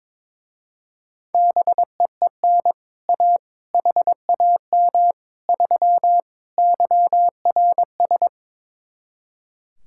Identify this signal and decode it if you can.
Morse code, heard as a keyed tone, about 22 wpm, 700 Hz. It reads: BEEN A HAM 3 YRS
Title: Speed-Racing Common QSO Phrases - 22wpm